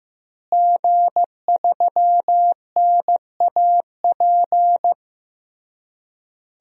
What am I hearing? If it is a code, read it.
Morse code, heard as a keyed tone, about 15 wpm, 700 Hz. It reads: G3NAP
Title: DXCC-100 Call Signs - 5 Characters - Repeated 3x - 15wpm